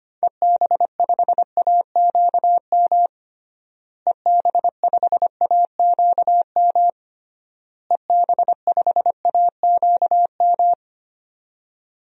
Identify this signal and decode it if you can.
Morse code, heard as a keyed tone, about 25 wpm, 700 Hz. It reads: EB5AQM EB5AQM EB5AQM
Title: DXCC-100 Call Signs - 6 Characters - Repeated 3x - 25wpm